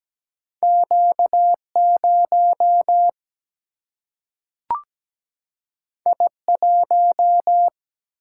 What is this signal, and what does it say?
Morse code, heard as a keyed tone, about 17 wpm, 700 Hz. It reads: Q0 I1
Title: Single-Letter Single-Number 17wpm